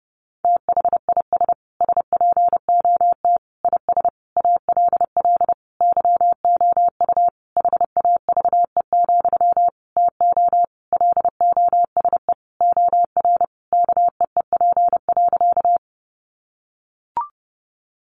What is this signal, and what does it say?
Morse code, heard as a keyed tone, about 30 wpm, 700 Hz. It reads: THIS SPOT IS ALL YOU HAVE, TO LOSE OR KEEP.
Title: Sentences from Top 1100 Words 30wpm